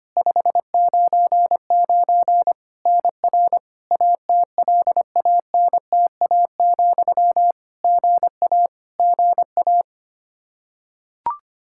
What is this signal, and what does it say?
Morse code, heard as a keyed tone, about 25 wpm, 700 Hz. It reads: 599 NR ATLANTA, GA GA
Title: Practice QSO - Exchange 1 - Beginning - 25wpm